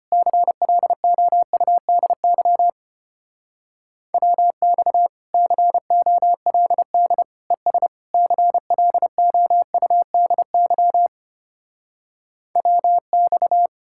Morse code, heard as a keyed tone, about 33 wpm, 700 Hz. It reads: CLOUDY WX COLD ES CLOUDY WX